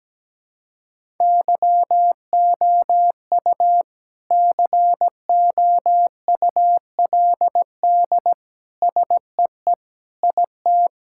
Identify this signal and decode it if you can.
Morse code, heard as a keyed tone, about 17 wpm, 700 Hz. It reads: YOU COULD SEE IT
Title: Sentences from Top 1000 Words - Review Part 5 - 17wpm